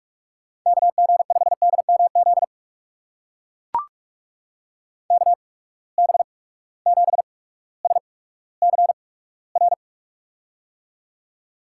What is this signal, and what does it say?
Morse code, heard as a keyed tone, about 44 wpm, spaced as for 19 wpm, 700 Hz. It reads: KG5DMZ KB7SCR
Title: USA Call Signs - 6 Characters - 8x Character Spacing - 45wpm